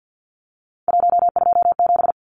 Morse code, heard as a keyed tone, about 50 wpm, 700 Hz. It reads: 127
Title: Numbers - 3-Digit 50wpm